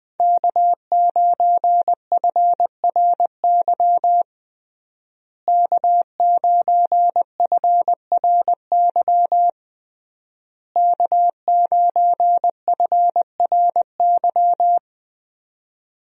Morse code, heard as a keyed tone, about 20 wpm, 700 Hz. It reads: K9FRY K9FRY K9FRY
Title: USA Call Signs - 5 Characters - Repeated 3x - 20wpm